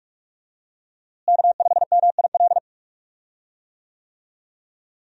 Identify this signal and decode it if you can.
Morse code, heard as a keyed tone, about 46 wpm, 700 Hz. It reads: K5MIL
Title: USA Call Signs - 5 Characters - Repeated 2x - 45wpm